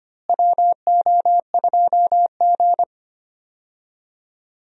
Morse code, heard as a keyed tone, about 25 wpm, 700 Hz. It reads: WO2G
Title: USA Call Signs - 4 Characters - Repeated 2x - 25wpm